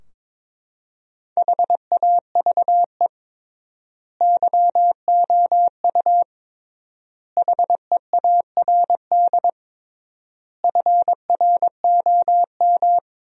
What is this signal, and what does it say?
Morse code, heard as a keyed tone, about 22 wpm, 700 Hz. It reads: HAVE YOU HEARD FROM
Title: 3x Word-Spacing - Sentences from Top 400 Words - 22wpm